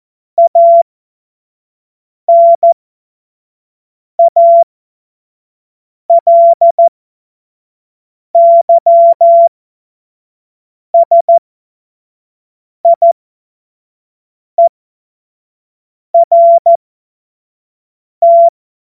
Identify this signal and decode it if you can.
Morse code, heard as a keyed tone, about 14 wpm, spaced as for 5 wpm, 680 Hz. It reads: ANALYSIERT
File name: Morsekurs / Witze / Witz 09. Februar 2021, 25 F70